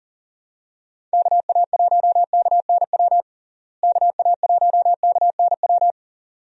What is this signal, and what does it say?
Morse code, heard as a keyed tone, about 40 wpm, 700 Hz. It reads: KA1KNW KA1KNW
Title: USA Call Signs - 6 Characters - Repeated 2x - 40wpm